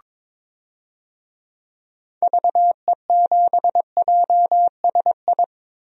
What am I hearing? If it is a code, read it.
Morse code, heard as a keyed tone, about 22 wpm, 700 Hz. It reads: VE7JSI
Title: DXCC-100 Call Signs - 6 Characters - Repeated 3x - 22wpm